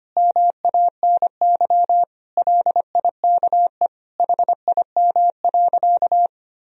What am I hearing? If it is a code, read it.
Morse code, heard as a keyed tone, about 25 wpm, 700 Hz. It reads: MANY LIKE HIM.
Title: Sentences from Top 1000 Words - Review Part 1 - 25wpm